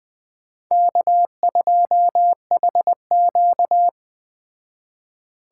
Morse code, heard as a keyed tone, about 20 wpm, 700 Hz. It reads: K2HQ
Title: Speed Racing - USA Call Signs 17wpm